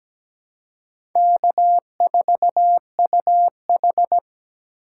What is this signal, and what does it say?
Morse code, heard as a keyed tone, about 17 wpm, 700 Hz. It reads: K4UH
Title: USA Call Signs - 4 Characters - 2x Character Spacing - 17wpm